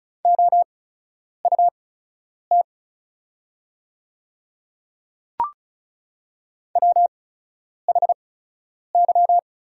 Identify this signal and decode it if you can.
Morse code, heard as a keyed tone, about 36 wpm, spaced as for 10 wpm, 700 Hz. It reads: OUT WHY
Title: 8x Character Spacing - Top 200 Words - 35wpm